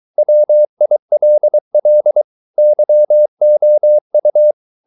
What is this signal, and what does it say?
Morse code, heard as a keyed tone, about 23 wpm, 600 Hz. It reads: WILL YOU